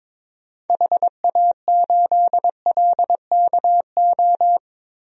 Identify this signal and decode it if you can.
Morse code, heard as a keyed tone, about 22 wpm, 700 Hz. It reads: HA8LKO
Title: DXCC-100 Call Signs - 6 Characters - 22wpm